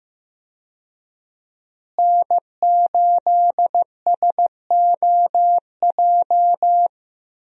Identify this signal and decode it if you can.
Morse code, heard as a keyed tone, about 15 wpm, 700 Hz. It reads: N8SOJ
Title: USA Call Signs - 5 Characters - Repeated 2x - 15wpm